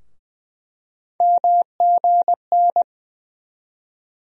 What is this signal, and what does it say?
Morse code, heard as a keyed tone, about 21 wpm, 700 Hz. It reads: MGN
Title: USA Call Sign Postfixes - 3 Letters 20wpm